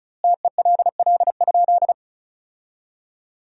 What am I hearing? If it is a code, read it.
Morse code, heard as a keyed tone, about 34 wpm, 700 Hz. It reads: TELL?
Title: 2x Word-Spacing - Sentences from Top 300 Words - 35wpm